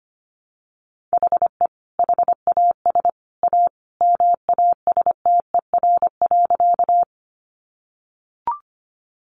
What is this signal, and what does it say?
Morse code, heard as a keyed tone, about 25 wpm, 700 Hz. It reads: HE HAS A MASTER.
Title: Sentences from Top 1300 Words 25wpm